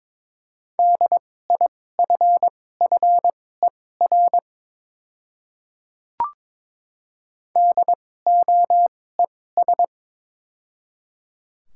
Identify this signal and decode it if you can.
Morse code, heard as a keyed tone, about 22 wpm, spaced as for 16 wpm, 700 Hz. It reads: DIFFER DOES
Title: 2x Character Spacing - Top 300 Words - 22wpm